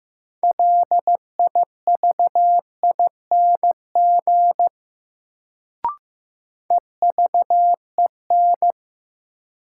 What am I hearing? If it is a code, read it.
Morse code, heard as a keyed tone, about 15 wpm, 700 Hz. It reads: LIVING EVEN
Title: Sets of 4 Words - New Word Drill - 15wpm